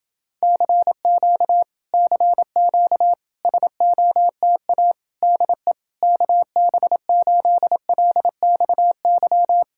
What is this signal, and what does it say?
Morse code, heard as a keyed tone, about 32 wpm, 700 Hz. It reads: CQ CQ SOTA DE KB8LXY